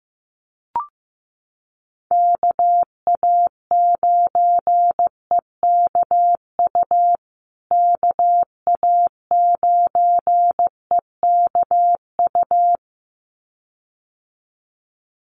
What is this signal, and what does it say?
Morse code, heard as a keyed tone, about 15 wpm, 700 Hz. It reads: KA9EKU KA9EKU
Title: USA Call Signs Repeated 2x 15wpm